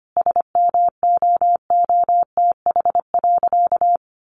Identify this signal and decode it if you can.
Morse code, heard as a keyed tone, about 25 wpm, 700 Hz. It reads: SMOOTH.
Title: Sentences from Top 1400 Words 25wpm